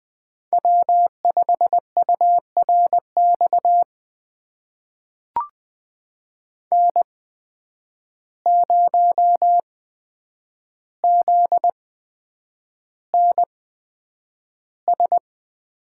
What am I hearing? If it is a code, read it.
Morse code, heard as a keyed tone, about 20 wpm, spaced as for 9 wpm, 700 Hz. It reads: W5URX N0ZNS
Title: USA Call Signs - 5 Characters - 8x Character Spacing - 20wpm